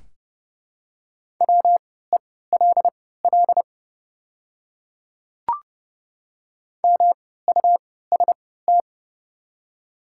Morse code, heard as a keyed tone, about 30 wpm, spaced as for 17 wpm, 700 Hz. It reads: WELL MUST
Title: 3x Character Spacing - Top 300 Words - 30wpm